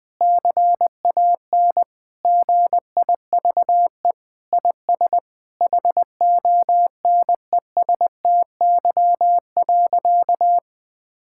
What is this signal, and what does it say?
Morse code, heard as a keyed tone, about 20 wpm, 700 Hz. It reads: CAN GIVE IS HONESTY.
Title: Sentences with 8-Word Phrases - 20wpm